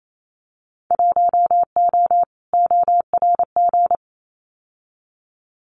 Morse code, heard as a keyed tone, about 28 wpm, 700 Hz. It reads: 1O ORG
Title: Winter Field Day Exchanges 28wpm